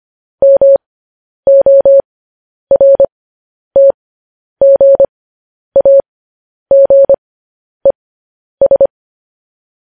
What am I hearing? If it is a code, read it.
Morse code, heard as a keyed tone, about 25 wpm, spaced as for 10 wpm, 550 Hz. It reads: MORTGAGES
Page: news headlines in morse code at 10 wpm